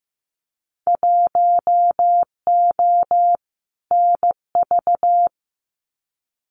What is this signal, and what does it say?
Morse code, heard as a keyed tone, about 15 wpm, 700 Hz. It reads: 1O NV
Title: Winter Field Day Exchanges Encore 15wpm